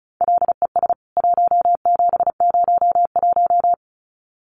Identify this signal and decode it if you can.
Morse code, heard as a keyed tone, about 34 wpm, 700 Hz. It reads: LES 1701
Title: CWT Exchange 35wpm